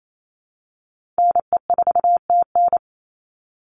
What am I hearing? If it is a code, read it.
Morse code, heard as a keyed tone, about 28 wpm, 700 Hz. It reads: NE4TN
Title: NA SOTA Chaser Callsigns 28wpm